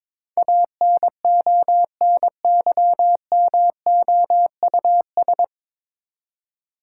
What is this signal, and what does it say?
Morse code, heard as a keyed tone, about 22 wpm, 700 Hz. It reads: ANONYMOUS
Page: Morse Code Ninja practice 22wpm